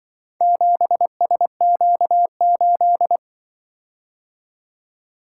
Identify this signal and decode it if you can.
Morse code, heard as a keyed tone, about 24 wpm, 700 Hz. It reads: 7SQ8
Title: Speed-Racing - Four Letter-Number Combinations - 20wpm